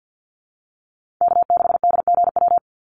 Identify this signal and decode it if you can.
Morse code, heard as a keyed tone, about 50 wpm, 700 Hz. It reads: K6DGW
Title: Call Signs Repeated 3x - 50wpm